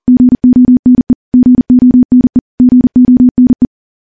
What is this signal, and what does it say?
Morse code, heard as a keyed tone, about 40 wpm, 250 Hz. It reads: GONE GONE GONE
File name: morse / morse-training-40WPM